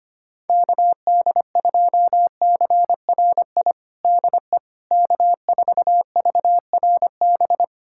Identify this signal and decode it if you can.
Morse code, heard as a keyed tone, about 25 wpm, 700 Hz. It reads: KD2CRI DE K4VRB
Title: USA Callsign DE Callsign 25wpm